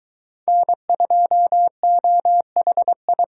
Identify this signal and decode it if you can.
Morse code, heard as a keyed tone, about 23 wpm, 700 Hz. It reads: N2OHI